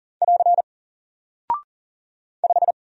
Morse code, heard as a keyed tone, about 40 wpm, 700 Hz. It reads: <AR> 5
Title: Single Character (Letter:Number:Punctuation:Prosign) 40wpm